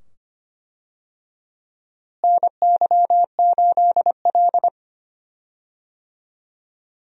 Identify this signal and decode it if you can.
Morse code, heard as a keyed tone, about 25 wpm, 700 Hz. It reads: NY8L
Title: USA Call Signs - 4 Characters - Repeated 2x - 25wpm